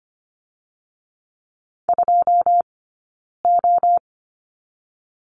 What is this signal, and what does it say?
Morse code, heard as a keyed tone, about 25 wpm, 700 Hz. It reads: 2 O
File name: Lesson 15d - Number 5 - Two Characters 25wpm